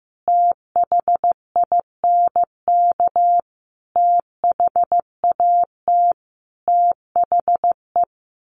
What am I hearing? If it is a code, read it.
Morse code, heard as a keyed tone, about 15 wpm, 700 Hz. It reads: THINK THAT THE
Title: Sets of 5 Words 15wpm